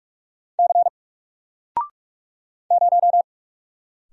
Morse code, heard as a keyed tone, about 48 wpm, 700 Hz. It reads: C 0
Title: Single Letter-Number Alt 45wpm